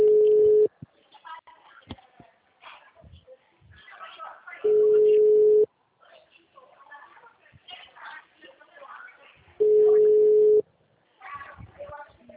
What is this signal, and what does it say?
Ringback tone (Europe, 425 Hz), with speech.
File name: out- -2008- - - .8